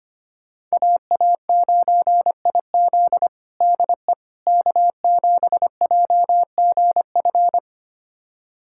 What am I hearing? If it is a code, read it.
Morse code, heard as a keyed tone, about 25 wpm, 700 Hz. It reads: AA9IZ DE K7JGF
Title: USA Callsign DE Callsign - 5 Characters - 25wpm